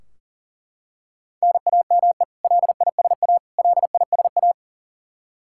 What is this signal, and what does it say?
Morse code, heard as a keyed tone, about 40 wpm, 700 Hz. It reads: NAME LISA LISA
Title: Practice QSO - Exchange 1 - Beginning - 40wpm